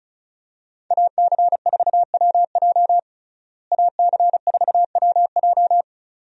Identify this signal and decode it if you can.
Morse code, heard as a keyed tone, about 36 wpm, 700 Hz. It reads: AC4WJ AC4WJ
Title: USA Call Signs - 5 Characters - Repeated 2x - 35wpm